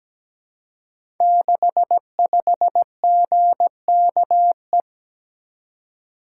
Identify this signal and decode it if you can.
Morse code, heard as a keyed tone, about 17 wpm, 690 Hz. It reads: 65GKE
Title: Code Groups - 5 Letter:Numbers - 17wpm